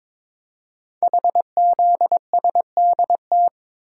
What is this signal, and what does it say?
Morse code, heard as a keyed tone, about 22 wpm, 700 Hz. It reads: HZSDT
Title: Code Groups - 5 Letters - 22wpm